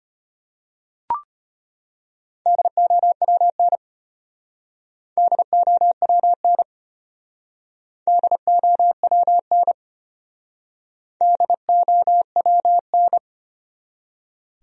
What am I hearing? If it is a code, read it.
Morse code, heard as a keyed tone, about 31 wpm, 700 Hz. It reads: DOWN DOWN DOWN DOWN
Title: Speed-Racing Top 100 Words 25wpm